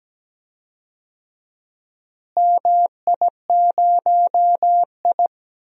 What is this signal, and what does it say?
Morse code, heard as a keyed tone, about 17 wpm, 700 Hz. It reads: MI0I
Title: DXCC-100 Call Signs - 4 Characters - Repeated 2x - 17wpm